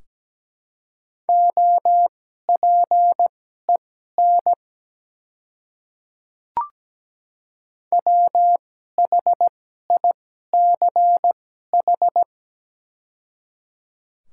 Morse code, heard as a keyed tone, about 17 wpm, spaced as for 12 wpm, 700 Hz. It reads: OPEN WHICH
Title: 2x Character Spacing - Top 300 Words - 17wpm